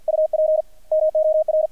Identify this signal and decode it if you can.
Morse code, heard as a keyed tone, about 50 wpm, 620 Hz. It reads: UP MOR
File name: archived / CRSnet